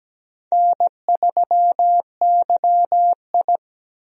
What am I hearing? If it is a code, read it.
Morse code, heard as a keyed tone, about 17 wpm, 700 Hz. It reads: N3YI
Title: USA Call Signs - 4 Characters - Repeated 2x - 17wpm